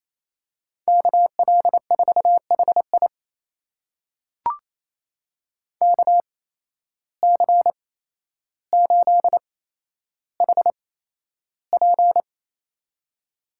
Morse code, heard as a keyed tone, about 28 wpm, spaced as for 12 wpm, 700 Hz. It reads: KL4HI KC8HP
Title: USA Call Signs - 5 Characters - 8x Character Spacing - 28wpm